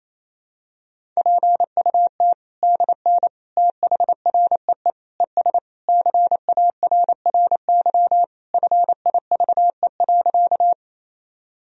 Morse code, heard as a keyed tone, about 27 wpm, 700 Hz. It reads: PUT DN THREE ES CARRY FIVE.
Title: Sentences from Top 400 Words and Abbreviations 28wpm